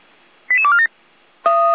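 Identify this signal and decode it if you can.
Five-tone selective call (ZVEI), heard as quick runs of five tones, tones of about 70 ms. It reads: 88247
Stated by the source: ZVEI five-tone selective call received by SDR.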